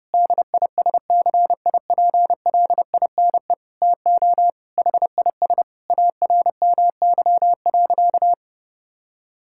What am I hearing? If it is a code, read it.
Morse code, heard as a keyed tone, about 30 wpm, 700 Hz. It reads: DISCIPLINE TO HIS ARMY.
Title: Sentences from Top 1000 Words - Review Part 6 - 30wpm